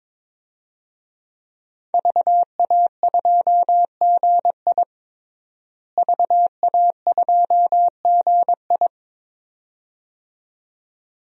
Morse code, heard as a keyed tone, about 22 wpm, 700 Hz. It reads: VA2GI VA2GI
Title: DXCC-100 Call Signs - 5 Characters - Repeated 2x - 22wpm